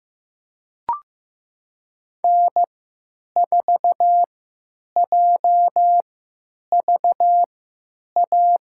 Morse code, heard as a keyed tone, about 15 wpm, spaced as for 9 wpm, 700 Hz. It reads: N4JVA
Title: USA Call Signs - 5 Characters - 3x Character Spacing - 15wpm